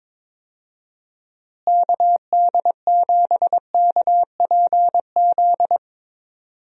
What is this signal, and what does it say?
Morse code, heard as a keyed tone, about 22 wpm, 700 Hz. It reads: KD7KPZ
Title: USA Call Signs - 6 Characters - Repeated 2x - 22wpm